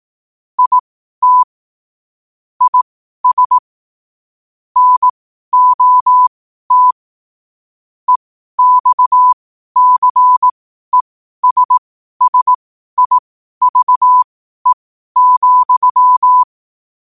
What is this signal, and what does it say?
Morse code, heard as a keyed tone, about 18 wpm, spaced as for 12 wpm, 1000 Hz. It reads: IT IS NOT EXCESSIVE,
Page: New quotes every day in morse code at 12 Words per minute.